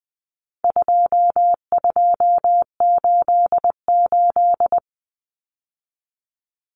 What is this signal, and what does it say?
Morse code, heard as a keyed tone, about 20 wpm, 700 Hz. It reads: 2288
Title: Common Cut Numbers - 4-Digit 20wpm